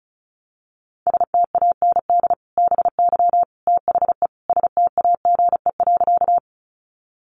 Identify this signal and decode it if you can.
Morse code, heard as a keyed tone, about 34 wpm, 700 Hz. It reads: STAND BY THE STAGE.
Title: Sentences from Top 1200 Words 35wpm